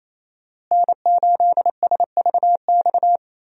Morse code, heard as a keyed tone, about 28 wpm, 700 Hz. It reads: N8SVX
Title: USA Call Signs - 5 Characters - Repeated 2x - 28wpm